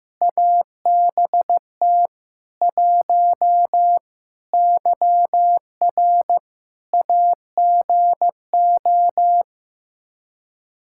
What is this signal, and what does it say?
Morse code, heard as a keyed tone, about 15 wpm, 700 Hz. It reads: ABT 1 YR AGO
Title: Practice QSO - Exchange 3 - More Info - 15wpm